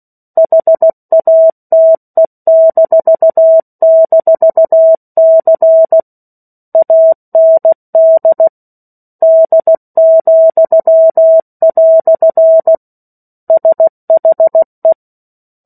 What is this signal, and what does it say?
Morse code, heard as a keyed tone, about 16 wpm, 650 Hz. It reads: HATE--C AND D," SHE
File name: 03-Chapter-03 16 WPM